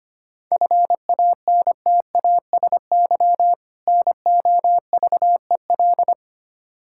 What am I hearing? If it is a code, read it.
Morse code, heard as a keyed tone, about 25 wpm, 700 Hz. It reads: FANTASY NOVEL
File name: practice / 25wpm / Difficult Words with Hints - 25wpm